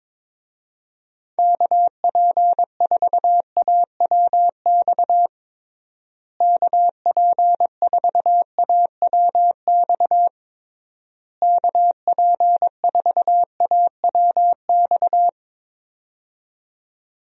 Morse code, heard as a keyed tone, about 22 wpm, 700 Hz. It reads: KP4AWX KP4AWX KP4AWX
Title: DXCC-100 Call Signs - 6 Characters - Repeated 3x - 22wpm